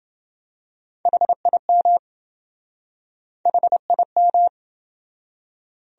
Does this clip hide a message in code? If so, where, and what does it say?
Yes: Morse code from 1.1 to 4.5 s, heard as a keyed tone, about 27 wpm, 700 Hz. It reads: HIM HIM